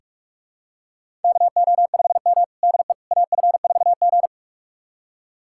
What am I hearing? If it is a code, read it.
Morse code, heard as a keyed tone, about 41 wpm, 700 Hz. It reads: KO5M DE AF4G
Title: USA Callsign DE Callsign - 4 Characters - 45wpm